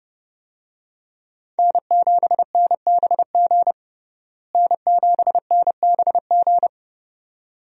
Morse code, heard as a keyed tone, about 32 wpm, 700 Hz. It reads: N7NBG N7NBG
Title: USA Call Signs - 5 Characters - Repeated 2x - 30wpm